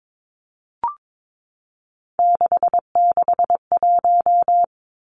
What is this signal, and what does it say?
Morse code, heard as a keyed tone, about 22 wpm, 700 Hz. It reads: 661
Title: Three Challenging Letter-Number Combinations 22wpm